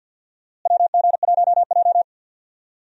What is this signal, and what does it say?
Morse code, heard as a keyed tone, about 47 wpm, 700 Hz. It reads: WG1J
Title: USA Call Signs - 4 Characters - Repeated 3x - 50wpm